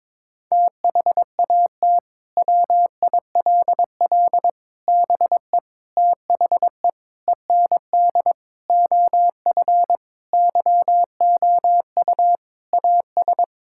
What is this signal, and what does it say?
Morse code, heard as a keyed tone, about 22 wpm, 700 Hz. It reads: THAT WILL BE THE END OF YOU AS
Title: Sentences from Top 1000 Words - Review Part 6 - 22wpm